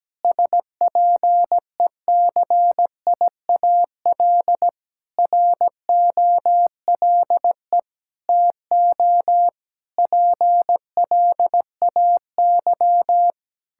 Morse code, heard as a keyed tone, about 17 wpm, 700 Hz. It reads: SPECIAL ROLE TO PLAY
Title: Sentences from Top 1000 Words - Review Part 6 - 17wpm